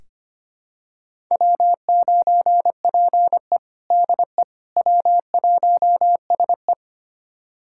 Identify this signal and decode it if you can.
Morse code, heard as a keyed tone, about 25 wpm, 700 Hz. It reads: W9PE DE W1SE
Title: USA Callsign DE Callsign - 4 Characters - 25wpm